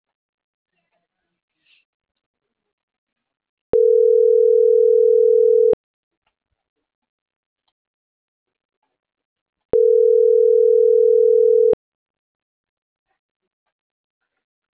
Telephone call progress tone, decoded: ringback tone

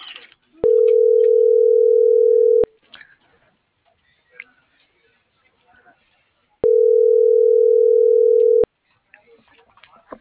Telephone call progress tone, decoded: ringback tone

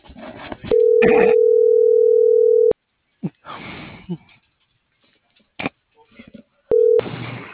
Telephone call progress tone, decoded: ringback tone

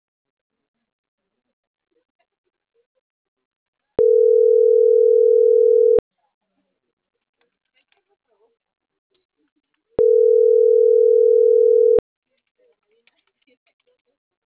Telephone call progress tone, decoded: ringback tone

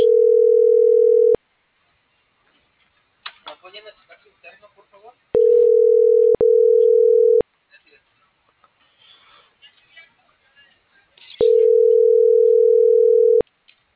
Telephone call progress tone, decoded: ringback tone

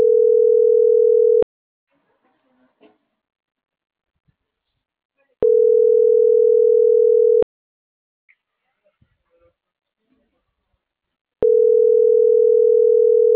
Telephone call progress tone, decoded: ringback tone

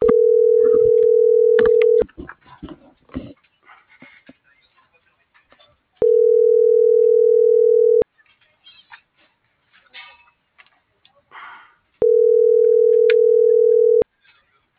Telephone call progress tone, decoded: ringback tone

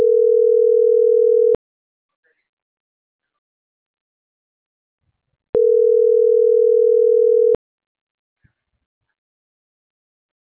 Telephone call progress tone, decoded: ringback tone

